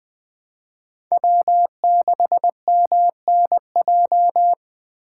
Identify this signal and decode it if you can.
Morse code, heard as a keyed tone, about 21 wpm, 700 Hz. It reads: W6MNJ
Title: Speed Racing - USA 1x3 Call Signs 17wpm